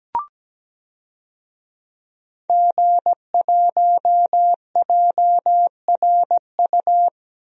Morse code, heard as a keyed tone, about 17 wpm, 700 Hz. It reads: G1JRU
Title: DXCC-100 Call Signs - 5 Characters - 17wpm